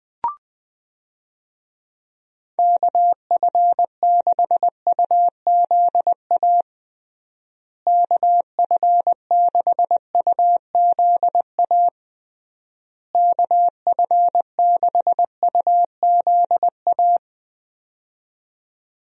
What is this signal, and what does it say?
Morse code, heard as a keyed tone, about 21 wpm, 700 Hz. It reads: KF6UZA KF6UZA KF6UZA
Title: USA Call Signs - 6 Characters - Repeated 3x - 20wpm